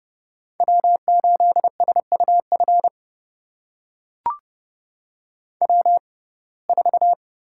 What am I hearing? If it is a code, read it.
Morse code, heard as a keyed tone, about 32 wpm, spaced as for 22 wpm, 700 Hz. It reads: W8SUF W4
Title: USA Call Signs - 5 Characters - 6x Character Spacing - 30wpm